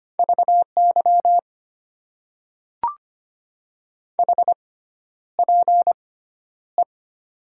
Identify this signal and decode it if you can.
Morse code, heard as a keyed tone, about 25 wpm, spaced as for 11 wpm, 700 Hz. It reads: VY HPE
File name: Top 90 QSO Elements - 6x Character Spacing - 25wpm